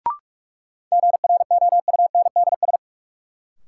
Morse code, heard as a keyed tone, about 46 wpm, 700 Hz. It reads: GROUNDS
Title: Sentences from Top 1300 Words - New Word Drill Alt - 45wpm